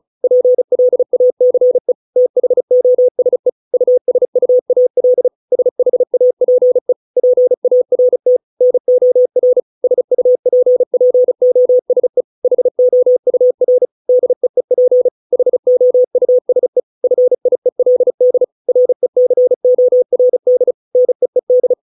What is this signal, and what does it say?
Morse code, heard as a keyed tone, about 34 wpm, 500 Hz. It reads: PLACE THOSE USUAL SHAPE PART NOR SUPPOSE HOUR DEEP HOUSE FIELD RECORD NEED